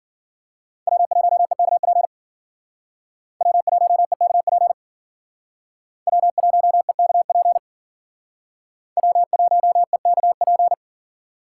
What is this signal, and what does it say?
Morse code, heard as a keyed tone, about 51 wpm, 700 Hz. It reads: W1EKP W1EKP W1EKP W1EKP